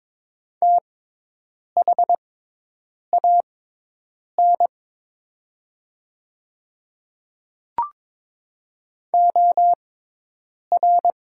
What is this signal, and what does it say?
Morse code, heard as a keyed tone, about 22 wpm, spaced as for 8 wpm, 690 Hz. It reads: THAN OR
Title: 6x Character Spacing - Top 100 Words - 22wpm